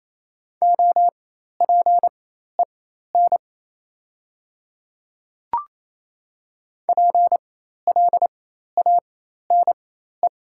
Morse code, heard as a keyed tone, about 27 wpm, spaced as for 13 wpm, 700 Hz. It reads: OPEN PLANE
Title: 4x Character Spacing - Top 500 Words - 28wpm